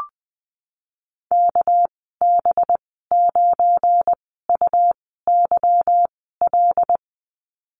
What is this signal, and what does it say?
Morse code, heard as a keyed tone, about 20 wpm, spaced as for 15 wpm, 700 Hz. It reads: KB9UYL
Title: 238 Lesson 38g - Letter q - Call Signs 20wpm